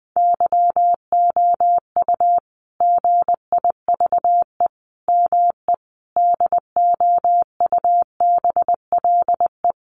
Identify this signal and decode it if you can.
Morse code, heard as a keyed tone, about 20 wpm, 700 Hz. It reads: YOU GIVE ME DOUBLE
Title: Sentences from Top 1400 Words 20wpm